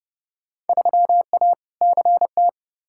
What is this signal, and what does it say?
Morse code, heard as a keyed tone, about 30 wpm, 700 Hz. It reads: 3A CT
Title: Field Day Exchanges - 30wpm